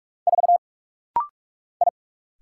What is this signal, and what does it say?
Morse code, heard as a keyed tone, about 46 wpm, 700 Hz. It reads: 4 I
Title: Single Letter-Number - Rapid-Fire 45wpm